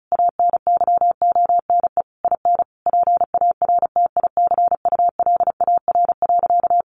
Morse code, heard as a keyed tone, about 34 wpm, 700 Hz. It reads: ANYONE IN PARTICULAR.
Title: Sentences from Top 900 Words 35wpm